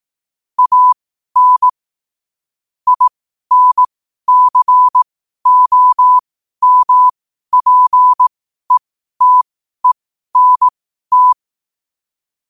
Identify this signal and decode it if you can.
Morse code, heard as a keyed tone, about 18 wpm, spaced as for 12 wpm, 1000 Hz. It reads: AN INCOMPETENT
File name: qotdfiles / 12wpm